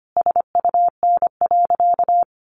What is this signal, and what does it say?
Morse code, heard as a keyed tone, about 25 wpm, 700 Hz. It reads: SUN.
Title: Sentences from Top 1300 Words 25wpm